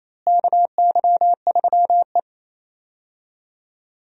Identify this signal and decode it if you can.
Morse code, heard as a keyed tone, about 28 wpm, 700 Hz. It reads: KY3E
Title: USA Call Signs - 4 Characters - Repeated 2x - 28wpm